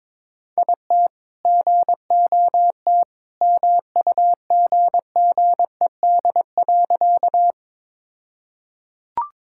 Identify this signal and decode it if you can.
Morse code, heard as a keyed tone, about 22 wpm, 700 Hz. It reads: IT GOT MUGGED.
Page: morse-code-ninja-practice-22wpm